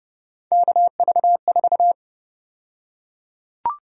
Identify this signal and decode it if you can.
Morse code, heard as a keyed tone, about 30 wpm, 700 Hz. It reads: KV4
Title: USA Call Sign Prefixes - 30wpm